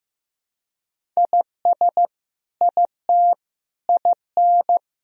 Morse code, heard as a keyed tone, about 15 wpm, 700 Hz. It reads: IS IT IN
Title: Sentences from Top 1000 Words - Review Part 4 - 15wpm